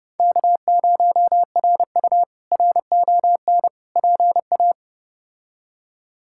Morse code, heard as a keyed tone, about 30 wpm, 700 Hz. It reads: K0RU RON PA
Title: K1USN SST QSO Practice - 30wpm